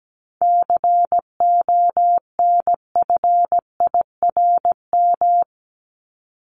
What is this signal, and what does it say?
Morse code, heard as a keyed tone, about 17 wpm, 700 Hz. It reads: CONFIRM
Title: Top 1751-2000 Words 17wpm